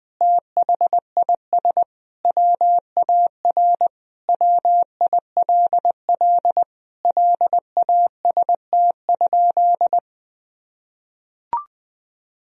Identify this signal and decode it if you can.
Morse code, heard as a keyed tone, about 20 wpm, 700 Hz. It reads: THIS WAR WILL LAST?
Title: Sentences from Top 1000 Words - Review Part 2 - 20wpm